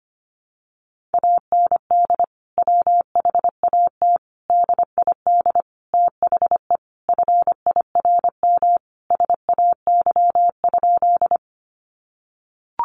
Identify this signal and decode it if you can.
Morse code, heard as a keyed tone, about 25 wpm, 700 Hz. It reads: AND WHAT DID THE FIRM SAY?
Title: Sentences from Top 1400 Words 25wpm